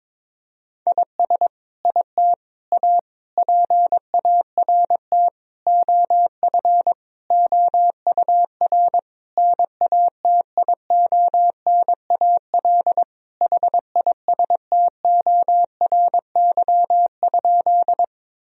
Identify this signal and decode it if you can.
Morse code, heard as a keyed tone, about 22 wpm, 700 Hz. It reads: IS IT A PART OF OUR NATIONAL HISTORY?
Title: Sentences from Top 1000 Words - Review Part 4 - 22wpm